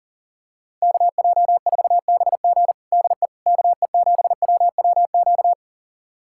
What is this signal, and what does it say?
Morse code, heard as a keyed tone, about 40 wpm, 700 Hz. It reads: KJ4BG DE KE7WWQ
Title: USA Callsign DE Callsign 40wpm